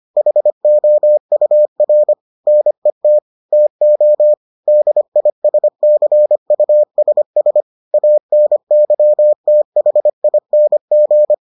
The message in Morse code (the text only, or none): HOUR NET TO DISCUSS ANYTHING